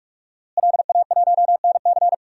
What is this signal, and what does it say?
Morse code, heard as a keyed tone, about 46 wpm, 700 Hz. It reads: LA1NC
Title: DXCC-100 Call Signs - 5 Characters - Repeated 2x - 45wpm